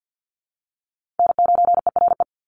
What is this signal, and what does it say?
Morse code, heard as a keyed tone, about 50 wpm, 700 Hz. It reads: N9ERE
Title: USA Call Signs 50wpm